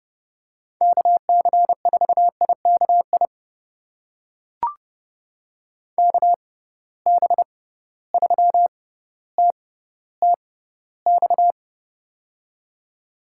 Morse code, heard as a keyed tone, about 30 wpm, spaced as for 15 wpm, 700 Hz. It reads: KC4IKI KB3TTX
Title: USA Call Signs - 6 Characters - 6x Character Spacing - 30wpm